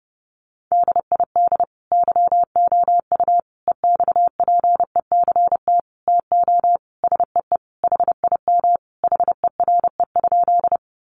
Morse code, heard as a keyed tone, about 30 wpm, 700 Hz. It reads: DID YOU EXPECT TO SEE HIM HERE?
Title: Sentences from Top 1100 Words 30wpm